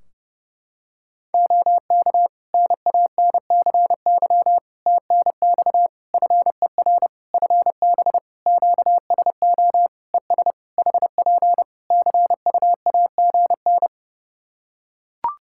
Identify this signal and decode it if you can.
Morse code, heard as a keyed tone, about 30 wpm, 700 Hz. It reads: OK NANCY TNX FER FB QSO ES HP CUAGN
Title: Practice QSO - Exchange 4 - Ending - 30wpm